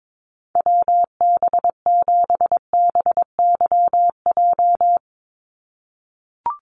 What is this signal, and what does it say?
Morse code, heard as a keyed tone, about 22 wpm, 700 Hz. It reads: WB7BYJ
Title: USA Call Signs 22wpm